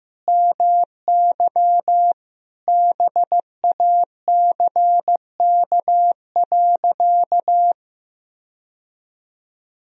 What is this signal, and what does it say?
Morse code, heard as a keyed tone, about 16 wpm, 700 Hz. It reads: MY BACK.
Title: Sentences from Top 1000 Words - Review Part 3 - 15wpm